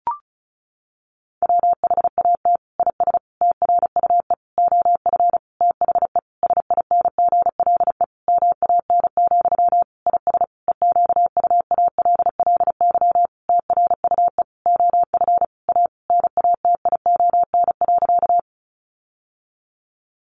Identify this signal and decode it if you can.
Morse code, heard as a keyed tone, about 34 wpm, 700 Hz. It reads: WHAT IS TRUE OF THE SINGLE MAN, IS EQUALLY TRUE OF A NATION.
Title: Sentences from Top 1400 Words 35wpm